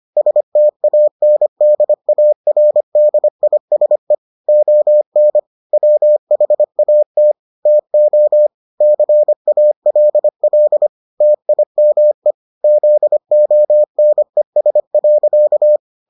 Morse code, heard as a keyed tone, about 25 wpm, 600 Hz. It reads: STANDARDISE ON WHAT TO CALL TIME ZONES.